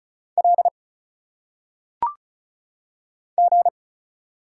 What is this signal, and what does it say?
Morse code, heard as a keyed tone, about 35 wpm, 700 Hz. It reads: L G
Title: Single Letters - Phonetic Pronunciation 35wpm